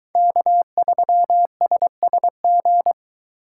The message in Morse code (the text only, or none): K3SSG